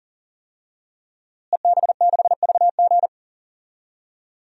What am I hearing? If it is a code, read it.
Morse code, heard as a keyed tone, about 40 wpm, 700 Hz. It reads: EB6VG
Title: DXCC-100 Call Signs - 5 Characters - Repeated 2x - 40wpm